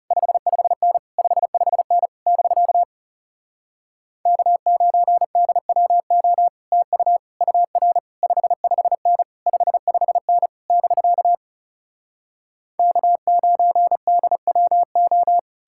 Morse code, heard as a keyed tone, about 37 wpm, 700 Hz. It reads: 55N 55N <BK> K9DWO TU UR 55N 55N <BK> K9DWO
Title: Speed-Racing Common QSO Phrases - 30wpm